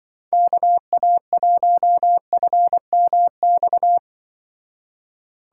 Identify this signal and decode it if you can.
Morse code, heard as a keyed tone, about 24 wpm, 700 Hz. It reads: KA1FMX
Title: Speed Racing - USA 2x3 Call Signs 20wpm